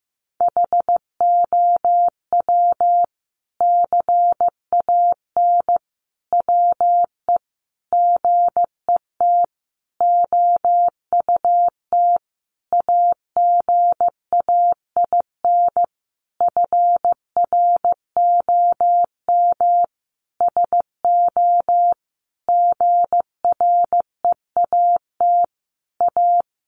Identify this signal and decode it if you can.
Morse code, heard as a keyed tone, about 15 wpm, 700 Hz. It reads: HOW CAN WE GET OUT AGAIN FROM SO GREAT A
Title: Sentences from Top 1200 Words 15wpm